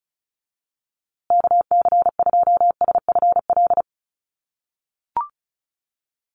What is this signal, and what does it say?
Morse code, heard as a keyed tone, about 34 wpm, 700 Hz. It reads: KC2SFL
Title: USA Call Signs Repeated 2x 35wpm